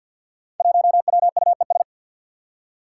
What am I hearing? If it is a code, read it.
Morse code, heard as a keyed tone, about 52 wpm, 700 Hz. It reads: 1WUES
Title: Five Challenging Letter-Number Combinations 50wpm